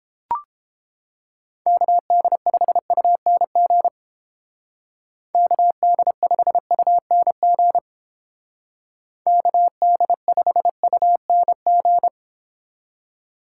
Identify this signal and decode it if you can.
Morse code, heard as a keyed tone, about 29 wpm, 700 Hz. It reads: KD5UNG KD5UNG KD5UNG